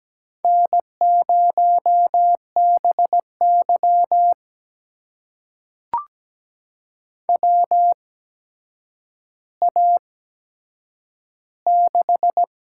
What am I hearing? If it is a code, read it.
Morse code, heard as a keyed tone, about 17 wpm, spaced as for 8 wpm, 700 Hz. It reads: N0BY WA6
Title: USA Call Signs - 8x Character Spacing - 17wpm